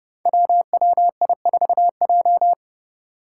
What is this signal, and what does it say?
Morse code, heard as a keyed tone, about 30 wpm, 700 Hz. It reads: WWI4J
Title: Five Challenging Letter-Number Combinations 30wpm